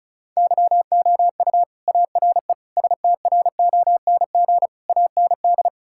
Morse code, heard as a keyed tone, about 36 wpm, 700 Hz. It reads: YOU ARE STRONG AND
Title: Sentences from Top 1000 Words - Review Part 5 - 35wpm